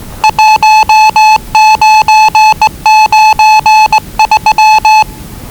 Morse code, heard as a keyed tone, about 20 wpm, 870 Hz. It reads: 1993
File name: WellingtonRadioZlwCloseDownOnHfAndMf / 01-Zlw 4277khz 417.5 qrt until 0925z